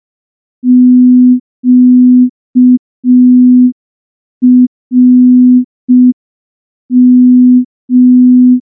Morse code, heard as a keyed tone, about 5 wpm, 250 Hz. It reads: QRM
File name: morse / morse-training-05WPM